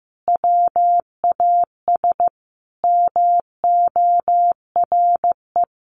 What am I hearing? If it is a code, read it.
Morse code, heard as a keyed tone, about 15 wpm, 700 Hz. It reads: WAS MORE
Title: Sentences from Top 1200 Words 15wpm